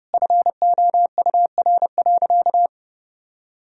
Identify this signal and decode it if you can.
Morse code, heard as a keyed tone, about 32 wpm, 700 Hz. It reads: FOUR.
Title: Sentences from Top 1000 Words - Review Part 1 - 30wpm